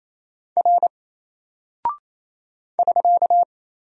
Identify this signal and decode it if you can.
Morse code, heard as a keyed tone, about 27 wpm, 700 Hz. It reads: R <SK>
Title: Single Character (Letter:Number:Punctuation:Prosign) Alt 28wpm